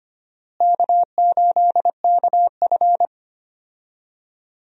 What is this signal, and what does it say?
Morse code, heard as a keyed tone, about 25 wpm, 700 Hz. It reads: K8KF
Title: USA Call Signs - 4 Characters - Repeated 2x - 25wpm